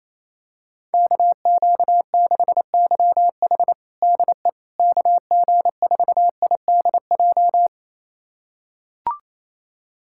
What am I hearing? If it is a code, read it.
Morse code, heard as a keyed tone, about 28 wpm, 700 Hz. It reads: KQ6YH DE KG4IDJ
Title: USA Callsign DE Callsign 28wpm